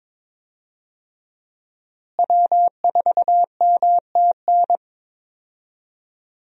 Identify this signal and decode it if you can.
Morse code, heard as a keyed tone, about 22 wpm, 700 Hz. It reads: W4MTN
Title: USA Call Signs - 5 Characters - Repeated 2x - 22wpm